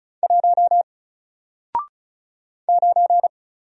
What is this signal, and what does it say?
Morse code, heard as a keyed tone, about 34 wpm, 700 Hz. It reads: 1 9
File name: Single-Letter-Number-35wpm